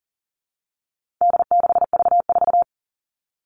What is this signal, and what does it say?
Morse code, heard as a keyed tone, about 40 wpm, 700 Hz. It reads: D6V4
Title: All Cut Numbers - 4-Digit 40wpm